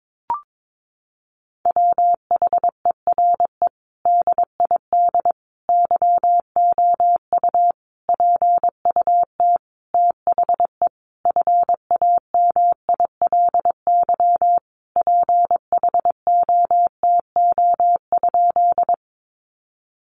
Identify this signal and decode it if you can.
Morse code, heard as a keyed tone, about 22 wpm, 700 Hz. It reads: WHERE DID YOU PUT THE FAMILY PHOTO?
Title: Sentences from Top 1200 Words 22wpm